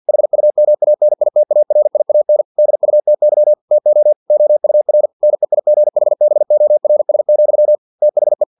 Morse code, heard as a keyed tone, about 50 wpm, 600 Hz. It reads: HUMANITARIAN DUTY TO OUR NEIGHBORS, THE
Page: Never 49wpm